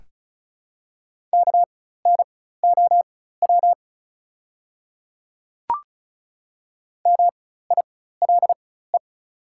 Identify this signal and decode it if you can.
Morse code, heard as a keyed tone, about 33 wpm, spaced as for 16 wpm, 690 Hz. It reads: KNOW MILE
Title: 4x Character Spacing - Top 500 Words - 35wpm